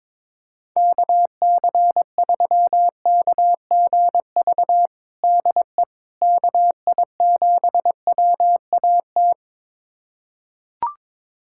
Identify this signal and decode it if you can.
Morse code, heard as a keyed tone, about 22 wpm, 700 Hz. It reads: KC3KGV DE KI7WAT
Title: USA Callsign DE Callsign 22wpm